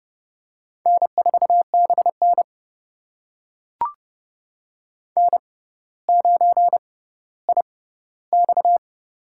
Morse code, heard as a keyed tone, about 30 wpm, spaced as for 15 wpm, 690 Hz. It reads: N4BN N9IX
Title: USA Call Signs - 4 Characters - 6x Character Spacing - 30wpm